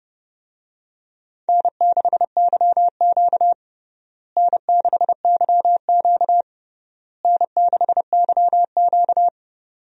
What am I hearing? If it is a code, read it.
Morse code, heard as a keyed tone, about 30 wpm, 700 Hz. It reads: N6YQ N6YQ N6YQ
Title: USA Call Signs - 4 Characters - Repeated 3x - 30wpm